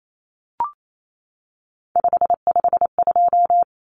Morse code, heard as a keyed tone, about 28 wpm, 700 Hz. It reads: 552
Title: Numbers - 3-Digit 28wpm